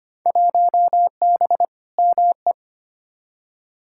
Morse code, heard as a keyed tone, about 25 wpm, 700 Hz. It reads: 1B ME
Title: Field Day Exchanges - 25wpm